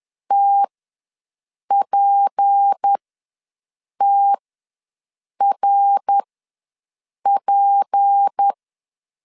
Morse code, heard as a keyed tone, about 10 wpm, spaced as for 6 wpm, 800 Hz. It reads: TPTRP
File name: LEZIONI / 6 PKX0 / 6 P ws 10wpm 3751